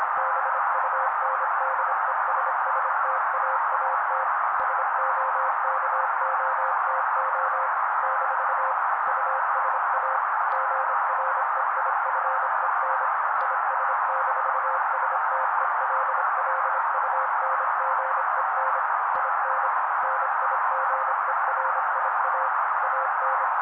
Morse code, heard as a keyed tone, about 25 wpm, 530 Hz. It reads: BUNDESSTAAT SOKOTO - USA GREIFEN IS-STELLUNGEN IN NIGERIA AN
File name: daily / 25wpm noise